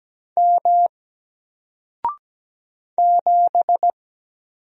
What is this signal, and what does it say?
Morse code, heard as a keyed tone, about 17 wpm, 700 Hz. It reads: M 7
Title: Single Letter-Number 17wpm